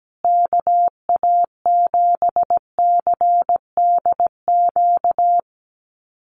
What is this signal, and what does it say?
Morse code, heard as a keyed tone, about 17 wpm, 700 Hz. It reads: KA7CDQ
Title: USA Call Signs Repeated 2x 17wpm